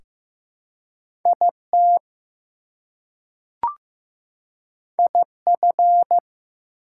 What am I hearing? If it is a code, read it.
Morse code, heard as a keyed tone, about 15 wpm, 700 Hz. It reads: IT IF
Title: Two Letter Words - 15wpm